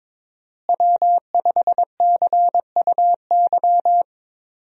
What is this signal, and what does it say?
Morse code, heard as a keyed tone, about 22 wpm, 700 Hz. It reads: W5CUY
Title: USA Call Signs - 5 Characters - Repeated 2x - 22wpm